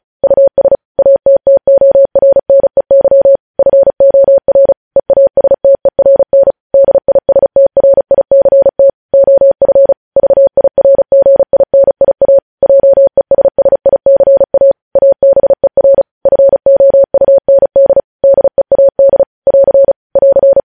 Morse code, heard as a keyed tone, about 36 wpm, 550 Hz. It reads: US ATTORNEY FOR EASTERN DISTRICT OF VIRGINIA JESSICA ABER FOUND DEAD <AR> <AR>
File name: podcast / News headlines-1x-35x35